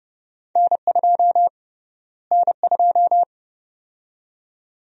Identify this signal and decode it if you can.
Morse code, heard as a keyed tone, about 29 wpm, 700 Hz. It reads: N2 N2
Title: USA Call Sign Prefixes - Repeated 2x - 30wpm